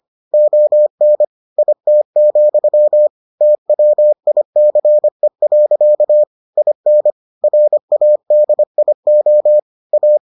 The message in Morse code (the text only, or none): ON IT, TWICE. IN RADIO A